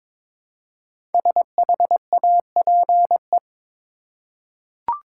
Morse code, heard as a keyed tone, about 22 wpm, 700 Hz. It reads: SHAPE
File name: 5 Letter Words - First 3 Letters Given - 22wpm